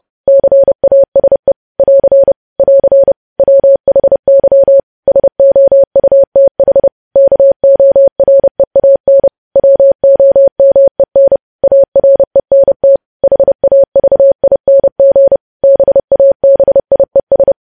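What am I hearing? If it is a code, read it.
Morse code, heard as a keyed tone, about 30 wpm, 550 Hz. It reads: CASE <AR> <AR> WHY SOUTH KOREAN WOMEN ARENT HAVING BABIES
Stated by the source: News headlines sent in Morse at 30 wpm, preceded by VVV.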